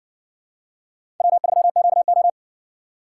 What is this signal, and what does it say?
Morse code, heard as a keyed tone, about 57 wpm, 700 Hz. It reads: W3QO